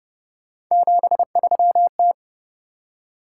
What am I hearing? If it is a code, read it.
Morse code, heard as a keyed tone, about 31 wpm, 700 Hz. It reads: 73T